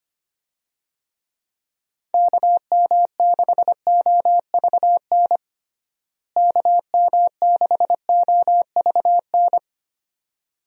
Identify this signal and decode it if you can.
Morse code, heard as a keyed tone, about 25 wpm, 700 Hz. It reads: KM6OVN KM6OVN
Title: USA Call Signs - 6 Characters - Repeated 2x - 25wpm